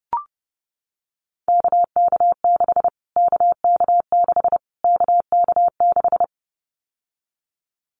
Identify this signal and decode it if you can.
Morse code, heard as a keyed tone, about 30 wpm, 700 Hz. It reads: KK6 KK6 KK6
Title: Callsign Prefixes for Top 25 DXCC - Repeated 3x - 30wpm